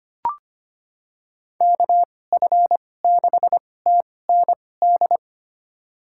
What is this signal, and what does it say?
Morse code, heard as a keyed tone, about 25 wpm, spaced as for 18 wpm, 700 Hz. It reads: KF6TND
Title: USA Call Signs - 6 Characters - 2x Character Spacing - 25wpm